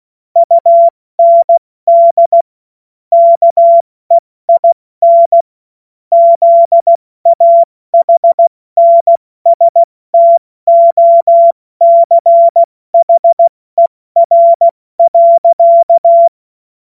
Morse code, heard as a keyed tone, about 16 wpm, 680 Hz. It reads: UND KEIN ZAHNSTOCHER.
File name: Morsekurs / Witze / Witz 02 Januar 2024 70 F80